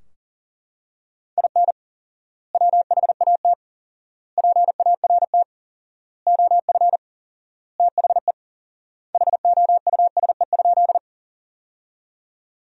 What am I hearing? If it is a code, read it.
Morse code, heard as a keyed tone, about 37 wpm, 700 Hz. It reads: IN WHAT PART OF THE HOUSE?
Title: 4x Word-Spacing - Sentences from Top 200 Words - 40wpm